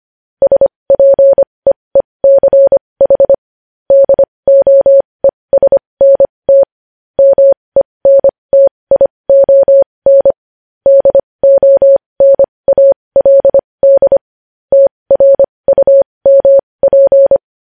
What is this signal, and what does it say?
Morse code, heard as a keyed tone, about 25 wpm, spaced as for 20 wpm, 550 Hz. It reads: SPEECH DOESNT MENTION DONALD TRUMP
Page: News Headlines in Morse Code at 20 WPM